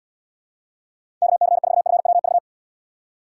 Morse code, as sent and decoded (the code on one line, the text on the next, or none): -.- -.-. ....- ..-. .-- ...-
KC4FWV